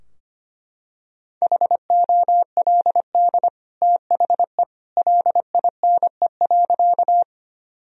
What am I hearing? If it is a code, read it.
Morse code, heard as a keyed tone, about 25 wpm, 700 Hz. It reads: HOLD THE LINE.
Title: Sentences from Top 1000 Words - Review Part 2 - 25wpm